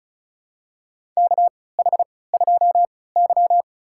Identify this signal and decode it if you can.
Morse code, heard as a keyed tone, about 36 wpm, spaced as for 20 wpm, 700 Hz. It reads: KH2Y
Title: USA Call Signs - 3x Character Spacing - 35wpm